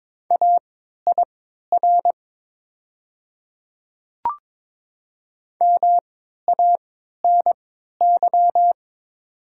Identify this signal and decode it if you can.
Morse code, heard as a keyed tone, about 22 wpm, spaced as for 12 wpm, 700 Hz. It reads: AIR MANY
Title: 3x Character Spacing - Top 200 Words - 22wpm